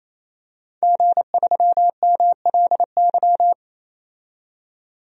Morse code, heard as a keyed tone, about 28 wpm, 700 Hz. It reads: G3MLY
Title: DXCC-100 Call Signs - 5 Characters - 28wpm